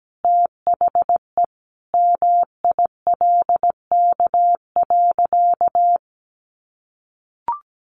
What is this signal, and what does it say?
Morse code, heard as a keyed tone, about 17 wpm, 700 Hz. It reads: THE MILK.
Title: Additional Easy Phrases 17wpm